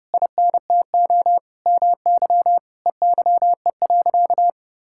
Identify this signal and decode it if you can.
Morse code, heard as a keyed tone, about 30 wpm, 700 Hz. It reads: INTO MY EYE.
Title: Sentences from Top 1600 Words 30wpm